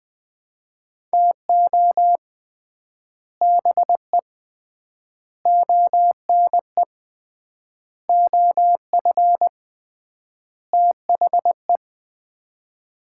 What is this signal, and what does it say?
Morse code, heard as a keyed tone, about 20 wpm, 700 Hz. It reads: TO BE ONE OF THE
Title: 3x Word-Spacing - Sets of 5 Words - 20wpm